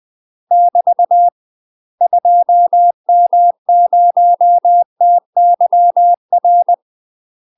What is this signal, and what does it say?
Morse code, heard as a keyed tone, about 20 wpm, 700 Hz. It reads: <BT> 2M0TYR
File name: UK MASTER QSOs TRAINING / 20WPM / UK MASTER QSOs 0041